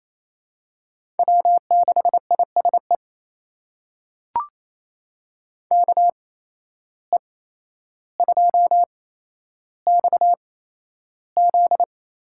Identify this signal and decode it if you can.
Morse code, heard as a keyed tone, about 28 wpm, spaced as for 12 wpm, 700 Hz. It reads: W6ISE KE2XZ
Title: USA Call Signs - 5 Characters - 8x Character Spacing - 28wpm